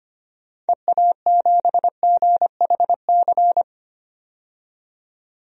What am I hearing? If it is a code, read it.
Morse code, heard as a keyed tone, about 25 wpm, 700 Hz. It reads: EA7GHC
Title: DXCC-100 Call Signs - 6 Characters - 25wpm